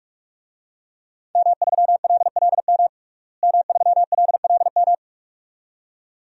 Morse code, heard as a keyed tone, about 42 wpm, 700 Hz. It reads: M3LLM M3LLM
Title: DXCC-100 Call Signs - 5 Characters - Repeated 2x - 45wpm